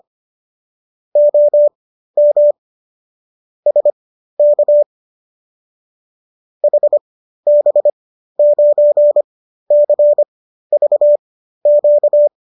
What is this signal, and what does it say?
Morse code, heard as a keyed tone, about 25 wpm, spaced as for 13 wpm, 570 Hz. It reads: OM SK HB9CVQ